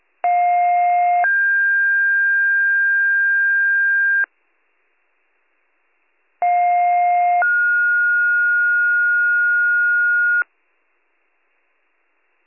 Two-tone page, heard as a one-second tone, then a three-second tone, about 710 then 1600 Hz, about 710 then 1430 Hz. It is heard from 0.2 to 10.4 s.